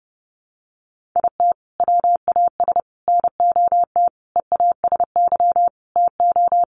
Morse code, heard as a keyed tone, about 30 wpm, 700 Hz. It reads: IT WAS NOT EASY TO
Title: Sentences from Top 1500 Words 30wpm